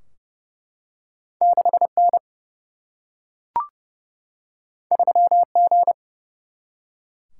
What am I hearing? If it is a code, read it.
Morse code, heard as a keyed tone, about 30 wpm, 700 Hz. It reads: 6 N 3 G
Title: Single-Letter Single-Number 30wpm